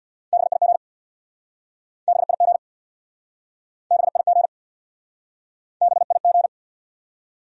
Morse code, heard as a keyed tone, about 68 wpm, 700 Hz. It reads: BIG BIG BIG BIG